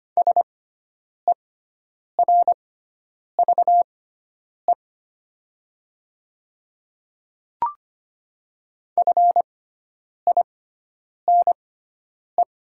Morse code, heard as a keyed tone, about 25 wpm, spaced as for 9 wpm, 700 Hz. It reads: SERVE FINE
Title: 6x Character Spacing - Top 500 Words - 25wpm